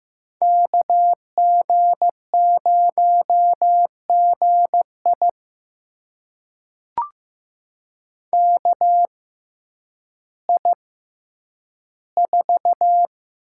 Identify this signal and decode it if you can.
Morse code, heard as a keyed tone, about 15 wpm, spaced as for 9 wpm, 700 Hz. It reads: KG0GI KI4
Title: USA Call Signs - 6x Character Spacing - 15wpm